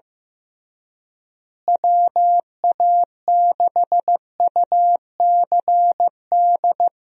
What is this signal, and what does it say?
Morse code, heard as a keyed tone, about 15 wpm, 700 Hz. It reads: WA6UCD
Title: USA Call Signs - 6 Characters - Repeated 3x - 15wpm